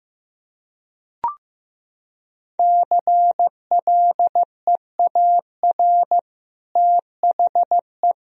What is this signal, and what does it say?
Morse code, heard as a keyed tone, about 15 wpm, 700 Hz. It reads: CLEAR THE
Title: Sentences from Top 1000 Words - Review Part 2 - 15wpm